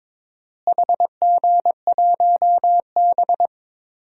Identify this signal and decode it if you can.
Morse code, heard as a keyed tone, about 22 wpm, 700 Hz. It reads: HG1B
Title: DXCC-100 Call Signs - 4 Characters - Repeated 3x - 22wpm